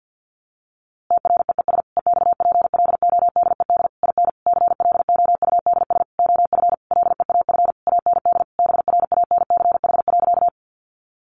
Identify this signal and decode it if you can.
Morse code, heard as a keyed tone, about 49 wpm, 700 Hz. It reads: TREES EXPLODED IN CLOUDS OF LEAF AND BRANCH.
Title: A Sound Of Thunder 50wpm